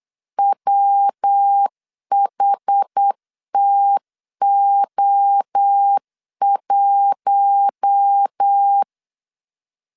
Morse code, heard as a keyed tone, about 8 wpm, 800 Hz. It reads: WHTO1
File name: LEZIONI / 6 PKX0 / 6 PKX0 8wpm 3341